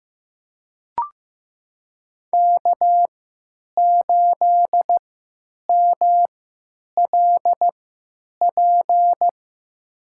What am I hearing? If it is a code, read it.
Morse code, heard as a keyed tone, about 15 wpm, spaced as for 9 wpm, 700 Hz. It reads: K8MLP
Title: USA Call Signs - 5 Characters - 3x Character Spacing - 15wpm